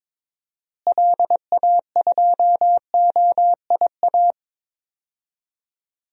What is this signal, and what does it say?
Morse code, heard as a keyed tone, about 22 wpm, 700 Hz. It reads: LA2OIA
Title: DXCC-100 Call Signs - 6 Characters - Repeated 2x - 22wpm